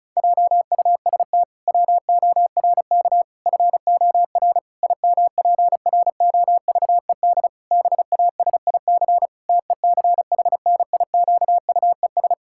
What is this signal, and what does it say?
Morse code, heard as a keyed tone, about 36 wpm, 700 Hz. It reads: JUST WORK FOR IMPROVED BASIC TECHNIQUES
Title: Pangram Challenge v2 - 35wpm